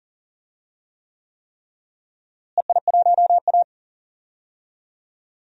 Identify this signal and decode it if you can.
Morse code, heard as a keyed tone, about 40 wpm, 700 Hz. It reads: EI1A
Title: DXCC-100 Call Signs - 4 Characters - Repeated 3x - 40wpm